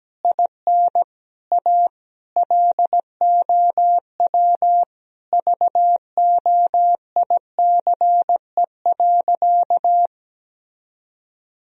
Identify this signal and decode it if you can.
Morse code, heard as a keyed tone, about 17 wpm, 700 Hz. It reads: IN A LOW VOICE.
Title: Sentences from Top 1000 Words - Review Part 1 - 17wpm